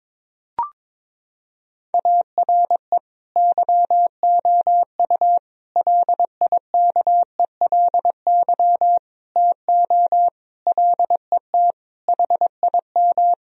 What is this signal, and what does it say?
Morse code, heard as a keyed tone, about 22 wpm, 700 Hz. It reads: ARE YOU LIKELY TO LET HIM
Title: Sentences from Top 1000 Words - Review Part 4 - 22wpm